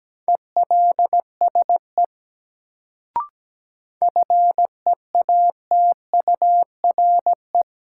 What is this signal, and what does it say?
Morse code, heard as a keyed tone, about 17 wpm, 700 Hz. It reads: ELSE FEATURE
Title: Sentences from Top 800 Words - New Word Drill - 17wpm